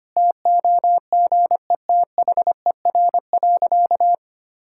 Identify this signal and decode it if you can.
Morse code, heard as a keyed tone, about 25 wpm, 690 Hz. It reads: TOGETHER.
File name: Easy Sentences with Hints - 25wpm